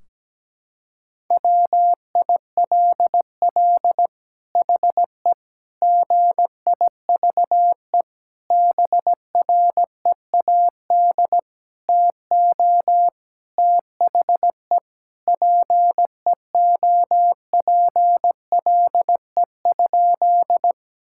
Morse code, heard as a keyed tone, about 17 wpm, 700 Hz. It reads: WILL HE GIVE BREAD TO THE PEOPLE?
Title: Sentences from Top 1600 Words 17wpm